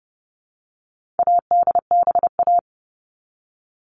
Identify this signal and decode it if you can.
Morse code, heard as a keyed tone, about 30 wpm, 700 Hz. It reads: ADBA
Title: All Cut Numbers - 4-Digit 30wpm